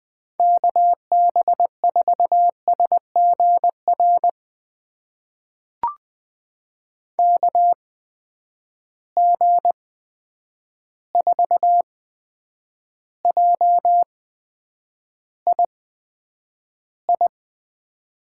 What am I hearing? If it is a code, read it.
Morse code, heard as a keyed tone, about 20 wpm, spaced as for 9 wpm, 700 Hz. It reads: KB4SGR KG4JII
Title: USA Call Signs - 6 Characters - 8x Character Spacing - 20wpm